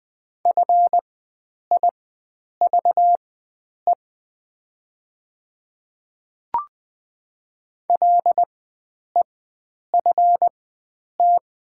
Morse code, heard as a keyed tone, about 20 wpm, spaced as for 9 wpm, 700 Hz. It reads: FIVE LEFT
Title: 4x Character Spacing - Top 500 Words - 20wpm